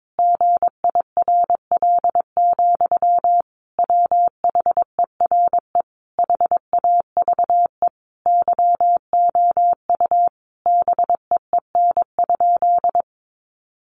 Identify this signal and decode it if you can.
Morse code, heard as a keyed tone, about 22 wpm, 700 Hz. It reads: GIRL, WHERE HAVE YOU BEEN?
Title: Sentences from Top 400 Words 22wpm